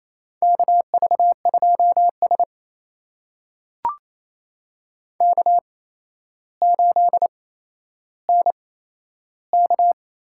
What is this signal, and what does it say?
Morse code, heard as a keyed tone, about 28 wpm, spaced as for 12 wpm, 700 Hz. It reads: KV2S K8NK
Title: USA Call Signs - 4 Characters - 8x Character Spacing - 28wpm